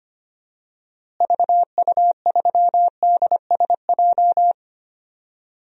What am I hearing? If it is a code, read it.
Morse code, heard as a keyed tone, about 25 wpm, 700 Hz. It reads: VU3DSJ
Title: DXCC-100 Call Signs - 6 Characters - Repeated 3x - 25wpm